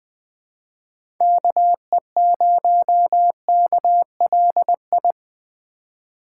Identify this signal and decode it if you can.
Morse code, heard as a keyed tone, about 20 wpm, 700 Hz. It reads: KE0KLI
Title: Speed Racing - USA 2x3 Call Signs 17wpm